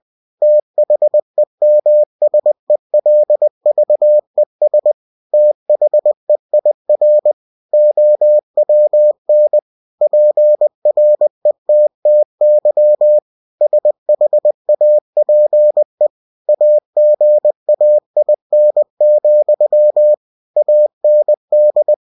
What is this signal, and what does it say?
Morse code, heard as a keyed tone, about 20 wpm, 600 Hz. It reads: THEMSELVES THEIR OWN PRETTY SHAPE AGAIN, AND
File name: cw-books / winnie-the-pooh / winnie-20wpm-page-0101